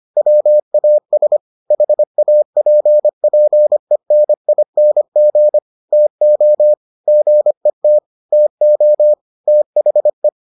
Morse code, heard as a keyed tone, about 25 wpm, 600 Hz. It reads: WAS HAPPENING TO GET TO THE